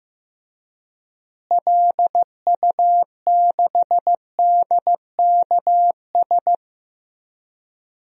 Morse code, heard as a keyed tone, about 15 wpm, 700 Hz. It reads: LU6DKS
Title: DXCC-100 Call Signs - 6 Characters - Repeated 3x - 15wpm